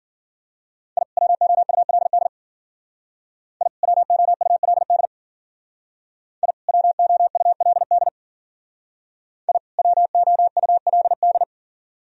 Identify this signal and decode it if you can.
Morse code, heard as a keyed tone, about 48 wpm, 700 Hz. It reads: I WOULD I WOULD I WOULD I WOULD
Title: Speed-Racing - Sets of 2 Words 40wpm